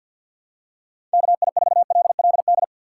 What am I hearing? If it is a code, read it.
Morse code, heard as a keyed tone, about 48 wpm, 700 Hz. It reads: KI4LLD
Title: USA Call Signs - 6 Characters - 3x Character Spacing - 50wpm